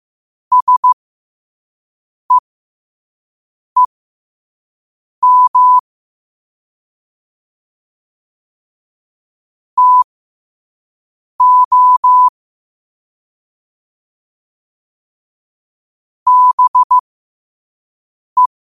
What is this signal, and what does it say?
Morse code, heard as a keyed tone, about 15 wpm, spaced as for 5 wpm, 1000 Hz. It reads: SEEM TO BE
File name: qotdfiles / 5wpm